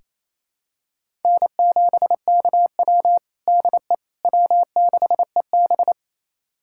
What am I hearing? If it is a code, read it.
Morse code, heard as a keyed tone, about 28 wpm, 700 Hz. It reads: N7KW DE W6EB
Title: USA Callsign DE Callsign - 4 Characters - 28wpm